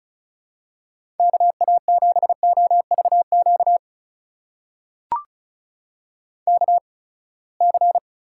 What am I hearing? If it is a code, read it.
Morse code, heard as a keyed tone, about 36 wpm, spaced as for 24 wpm, 700 Hz. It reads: KA7OVQ KC
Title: USA Call Signs - 6 Characters - 8x Character Spacing - 35wpm